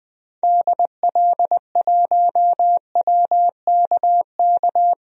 Morse code, heard as a keyed tone, about 21 wpm, 700 Hz. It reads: DL1WKK
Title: DXCC-100 Call Signs - 6 Characters - 20wpm